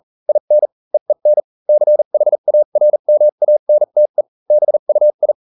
Morse code, heard as a keyed tone, about 40 wpm, spaced as for 35 wpm, 600 Hz. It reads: IN EEN CHARMANTE BUI